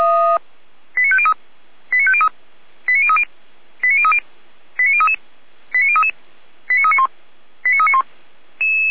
Five-tone selective call (ZVEI), heard as quick runs of five tones, tones of about 80 ms. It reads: 89503 89503 89030 89030 89033 89033 89391 89391
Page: SDR/ZVEI